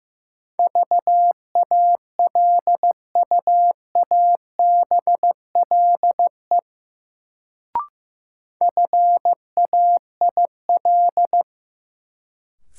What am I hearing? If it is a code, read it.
Morse code, heard as a keyed tone, about 15 wpm, 700 Hz. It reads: VALUABLE FAIL
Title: Sentences from Top 1100 Words - New Word Drill - 15wpm